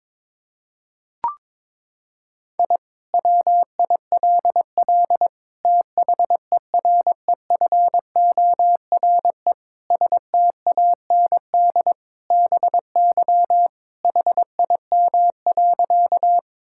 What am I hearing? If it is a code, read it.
Morse code, heard as a keyed tone, about 21 wpm, 700 Hz. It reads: I WILL THEREFORE STAND BY HIM.
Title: Sentences from Top 1000 Words - Review Part 5 - 22wpm